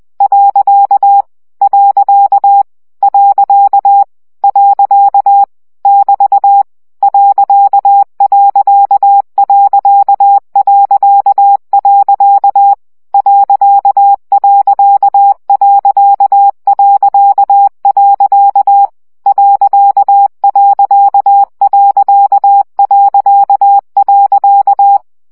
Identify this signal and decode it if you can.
Morse code, heard as a keyed tone, about 20 wpm, 800 Hz. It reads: . . <BT> ..... ..... .....
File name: CW gyakorlas pont 17 baud